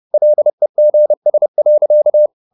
Morse code, heard as a keyed tone, about 30 wpm, 600 Hz. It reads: LEGS.
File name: the old man and the sea / the old man and the sea 1 30wpm